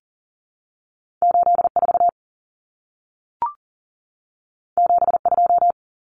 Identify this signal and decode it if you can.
Morse code, heard as a keyed tone, about 40 wpm, 700 Hz. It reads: 84 72
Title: Common Cut Numbers - 2-Digit 40wpm